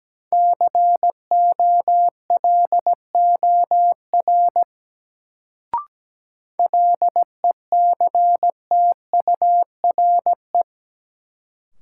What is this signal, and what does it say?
Morse code, heard as a keyed tone, about 17 wpm, 700 Hz. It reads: COLOR LECTURE
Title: Sentences from Top 1600 Words - New Word Drill Alt - 17wpm